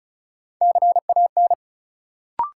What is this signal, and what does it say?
Morse code, heard as a keyed tone, about 36 wpm, 700 Hz. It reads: CAN
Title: Sets of 2 Words - New Word Drill - 35wpm